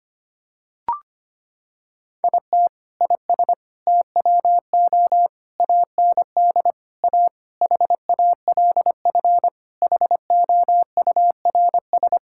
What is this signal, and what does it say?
Morse code, heard as a keyed tone, about 25 wpm, 700 Hz. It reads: IT IS TWO AND A HALF HOURS
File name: practice / 25wpm / Sentences with 4-Word Phrases - 25wpm